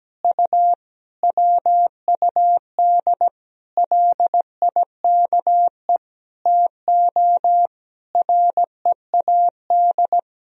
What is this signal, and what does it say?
Morse code, heard as a keyed tone, about 17 wpm, 700 Hz. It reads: U WUD LIKE TO READ
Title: Sentences from Top 500 Words and Abbreviations 17wpm